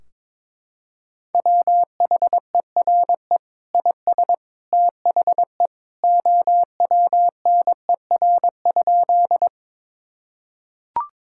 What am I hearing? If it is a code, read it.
Morse code, heard as a keyed tone, about 22 wpm, 700 Hz. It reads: WHERE IS THE OWNER?
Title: Sentences from Top 1600 Words 22wpm